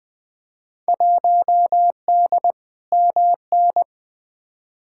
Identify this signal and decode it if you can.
Morse code, heard as a keyed tone, about 20 wpm, 700 Hz. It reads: 1D MN
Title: Field Day Exchanges - 20wpm